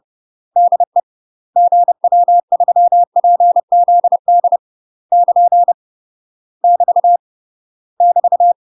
Morse code, heard as a keyed tone, about 30 wpm, 700 Hz. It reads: DE GW3PZD <KN> <BT> <BT>
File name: UK MASTER QSOs TRAINING / 30WPM / UK MASTER QSOs 0026